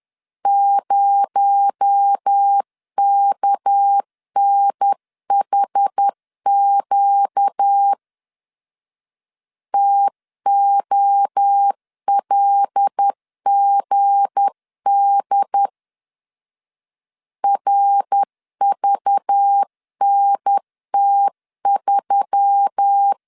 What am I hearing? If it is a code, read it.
Morse code, heard as a keyed tone, about 10 wpm, 800 Hz. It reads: 0KNHQ TOLGD RVNT3
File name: LEZIONI / 7 GF9Z / 7 G 10wpm 2501